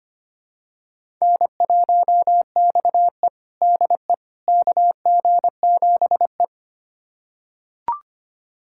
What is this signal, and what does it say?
Morse code, heard as a keyed tone, about 25 wpm, 700 Hz. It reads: N1XE DE KG7E
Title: USA Callsign DE Callsign - 4 Characters - 25wpm